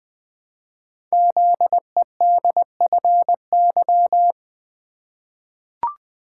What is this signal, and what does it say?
Morse code, heard as a keyed tone, about 20 wpm, 700 Hz. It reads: ZEDFY
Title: Code Groups - 5 Letters - 20wpm